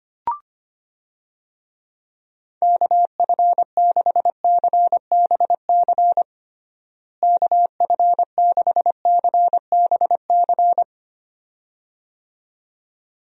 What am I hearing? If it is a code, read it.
Morse code, heard as a keyed tone, about 25 wpm, 700 Hz. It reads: KF6CBC KF6CBC
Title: USA Call Signs - 6 Characters - Repeated 2x - 25wpm